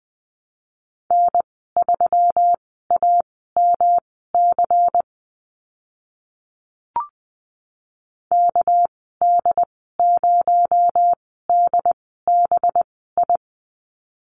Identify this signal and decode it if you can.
Morse code, heard as a keyed tone, about 20 wpm, spaced as for 14 wpm, 700 Hz. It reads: N3AMC KD0DBI
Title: 245 Lesson 39g - Letter z - Call Signs 20wpm